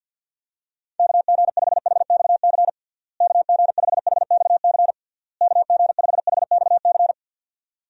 Morse code, heard as a keyed tone, about 49 wpm, 700 Hz. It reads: KG5HXC KG5HXC KG5HXC
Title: USA Call Signs - 6 Characters - Repeated 3x - 50wpm